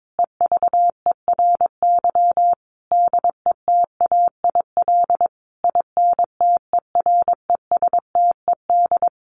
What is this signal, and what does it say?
Morse code, heard as a keyed tone, about 22 wpm, 700 Hz. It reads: EVERY DETAIL INTERESTED
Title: Sentences from Top 1500 Words 22wpm